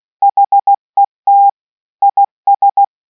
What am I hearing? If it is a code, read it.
Morse code, heard as a keyed tone, about 16 wpm, 800 Hz. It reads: HET IS